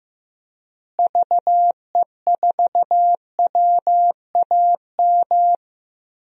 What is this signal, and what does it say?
Morse code, heard as a keyed tone, about 15 wpm, 700 Hz. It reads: VE4WAM
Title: DXCC-100 Call Signs - 6 Characters - Repeated 3x - 15wpm